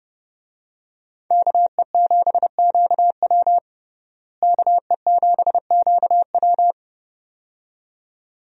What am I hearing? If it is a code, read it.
Morse code, heard as a keyed tone, about 30 wpm, 700 Hz. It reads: KE7QW KE7QW
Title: USA Call Signs - 5 Characters - Repeated 2x - 30wpm